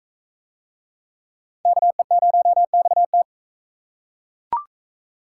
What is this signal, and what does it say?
Morse code, heard as a keyed tone, about 41 wpm, 700 Hz. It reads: KE0XT